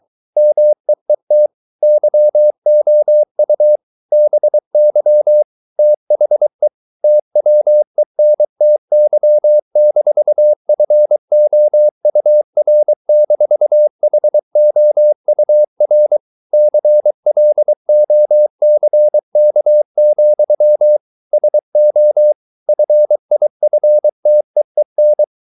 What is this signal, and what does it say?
Morse code, heard as a keyed tone, about 23 wpm, 600 Hz. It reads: MEET YOU BY THE TWENTY-FOUR-HOUR CLOCK, SO FIFTEEN